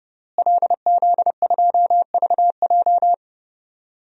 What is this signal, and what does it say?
Morse code, heard as a keyed tone, about 30 wpm, 700 Hz. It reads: LZ2VJ
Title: DXCC-100 Call Signs - 5 Characters - Repeated 2x - 30wpm